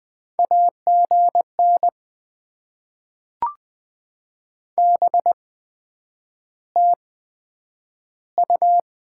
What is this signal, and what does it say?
Morse code, heard as a keyed tone, about 20 wpm, spaced as for 9 wpm, 700 Hz. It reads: AGN BTU
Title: Top 20 QSO Elements - 8x Character Spacing - 20wpm